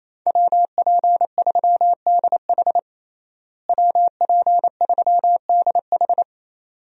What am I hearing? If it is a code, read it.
Morse code, heard as a keyed tone, about 28 wpm, 700 Hz. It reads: WP3DH WP3DH
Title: USA Call Signs - 5 Characters - Repeated 2x - 28wpm